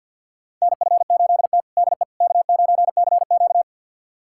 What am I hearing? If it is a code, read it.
Morse code, heard as a keyed tone, about 50 wpm, 700 Hz. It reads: NF8T DE K9CQ
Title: USA Callsign DE Callsign - 4 Characters - 50wpm